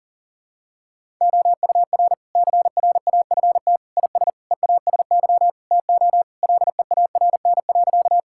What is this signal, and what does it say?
Morse code, heard as a keyed tone, about 40 wpm, 700 Hz. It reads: OUR CRAFT IS EASY TO LEARN.
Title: Sentences from Top 1000 Words - Review Part 6 - 40wpm